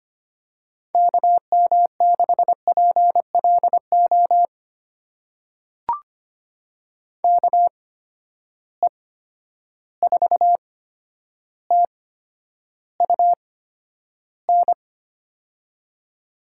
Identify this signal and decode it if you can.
Morse code, heard as a keyed tone, about 25 wpm, spaced as for 11 wpm, 700 Hz. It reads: KM6PLO KE4TUN
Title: USA Call Signs - 6 Characters - 8x Character Spacing - 25wpm